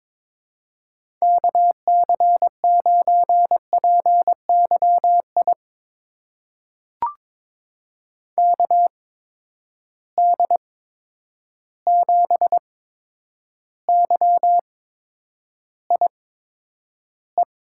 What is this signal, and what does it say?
Morse code, heard as a keyed tone, about 21 wpm, spaced as for 9 wpm, 700 Hz. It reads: KC9PYI KD7YIE
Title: USA Call Signs - 6 Characters - 8x Character Spacing - 22wpm